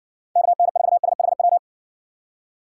Morse code, heard as a keyed tone, about 59 wpm, 700 Hz. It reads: KN4SHW